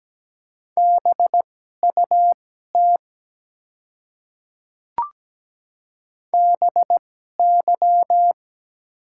Morse code, heard as a keyed tone, about 17 wpm, spaced as for 12 wpm, 700 Hz. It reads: BUT BY
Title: 2x Character Spacing - Top 100 Words - 17wpm